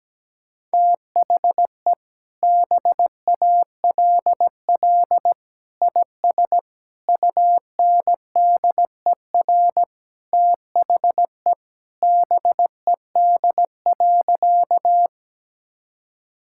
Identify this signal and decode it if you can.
Morse code, heard as a keyed tone, about 17 wpm, 700 Hz. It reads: THE BALL IS UNDER THE BED.
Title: Sentences from Top 1000 Words - Review Part 6 - 17wpm